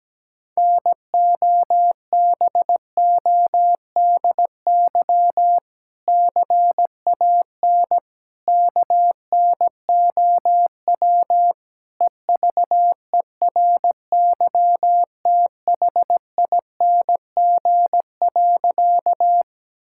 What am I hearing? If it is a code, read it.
Morse code, heard as a keyed tone, about 17 wpm, 700 Hz. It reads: NOBODY CAN KNOW EVERYTHING.
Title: Sentences from Top 1600 Words 17wpm